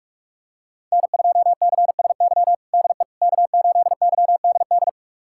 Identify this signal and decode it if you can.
Morse code, heard as a keyed tone, about 47 wpm, 700 Hz. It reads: N2CSY DE K8YDD
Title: USA Callsign DE Callsign - 5 Characters - 45wpm